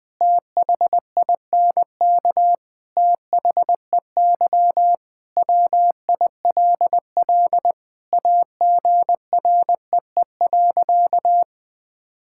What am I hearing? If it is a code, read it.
Morse code, heard as a keyed tone, about 21 wpm, 700 Hz. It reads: THINK THEY WILL AGREE.
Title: Sentences with 5-Word Phrases - 20wpm